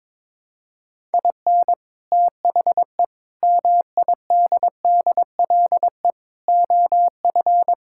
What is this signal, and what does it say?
Morse code, heard as a keyed tone, about 22 wpm, 700 Hz. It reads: IN THE MIDDLE OF
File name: Sentences with 4-Word Phrases - 22wpm